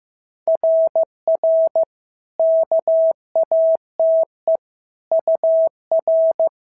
Morse code, heard as a keyed tone, about 15 wpm, 650 Hz. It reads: RR KATE UR
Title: SOTA QSO Practice 15wpm